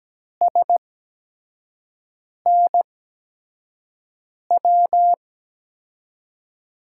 Morse code, heard as a keyed tone, about 17 wpm, spaced as for 5 wpm, 700 Hz. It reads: SNW
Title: USA Call Signs - 8x Character Spacing - 17wpm